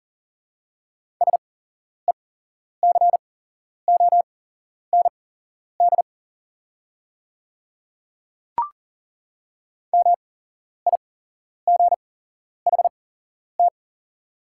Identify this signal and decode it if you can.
Morse code, heard as a keyed tone, about 41 wpm, spaced as for 11 wpm, 700 Hz. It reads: SECOND MIGHT
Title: 8x Character Spacing - Top 300 Words - 40wpm